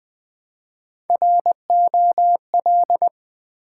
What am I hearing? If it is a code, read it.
Morse code, heard as a keyed tone, about 20 wpm, 700 Hz. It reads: ROL
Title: Build A Word - 1 Letter At A Time - 4-Letter Words - 20wpm